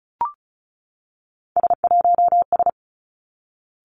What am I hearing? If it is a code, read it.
Morse code, heard as a keyed tone, about 36 wpm, 700 Hz. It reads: S1S
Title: Three Challenging Letter-Number Combinations 35wpm